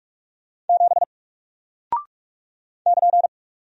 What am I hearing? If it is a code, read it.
Morse code, heard as a keyed tone, about 44 wpm, 700 Hz. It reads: 7 <KN>
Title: Single Character (Letter:Number:Punctuation:Prosign) Alt 45wpm